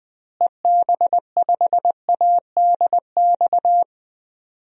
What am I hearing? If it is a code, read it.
Morse code, heard as a keyed tone, about 20 wpm, 700 Hz. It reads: EB5ADX
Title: DXCC-100 Call Signs - 6 Characters - Repeated 2x - 20wpm